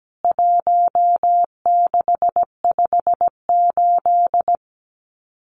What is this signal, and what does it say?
Morse code, heard as a keyed tone, about 17 wpm, 700 Hz. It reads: 1658
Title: CWT Exchange 17wpm